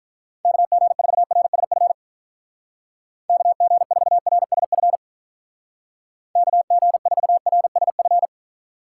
Morse code, heard as a keyed tone, about 42 wpm, 700 Hz. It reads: KG4RSF KG4RSF KG4RSF
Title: Speed Racing - USA 2x3 Call Signs 35wpm